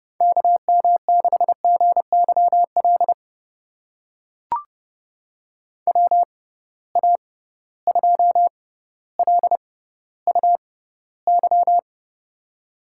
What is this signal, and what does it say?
Morse code, heard as a keyed tone, about 31 wpm, spaced as for 15 wpm, 700 Hz. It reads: KM6GYL WA2LUY
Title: USA Call Signs - 6 Characters - 6x Character Spacing - 30wpm